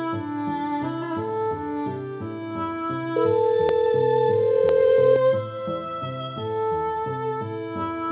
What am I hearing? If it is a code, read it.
Ringback tone (North America), 440 + 480 Hz.